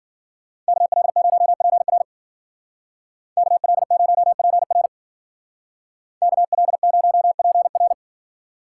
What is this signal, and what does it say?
Morse code, heard as a keyed tone, about 50 wpm, 700 Hz. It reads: KL0PR KL0PR KL0PR
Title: Speed Racing - USA 2x2 Call Signs 40wpm